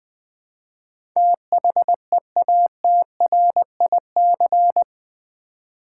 Morse code, heard as a keyed tone, about 20 wpm, 700 Hz. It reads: THEATRIC
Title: Build A Word - 4 Letters At A Time - 10-Letter Words - 20wpm